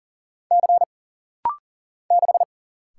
Morse code, heard as a keyed tone, about 40 wpm, 700 Hz. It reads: C 6
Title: Single Letter-Number - Rapid-Fire 40wpm